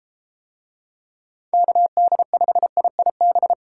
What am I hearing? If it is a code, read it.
Morse code, heard as a keyed tone, about 34 wpm, 700 Hz. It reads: KD5IIB